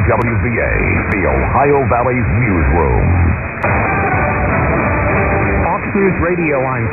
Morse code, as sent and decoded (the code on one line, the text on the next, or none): none